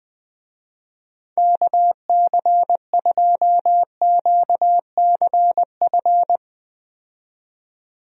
Morse code, heard as a keyed tone, about 20 wpm, 700 Hz. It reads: KC2QCF
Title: USA Call Signs - 6 Characters - 20wpm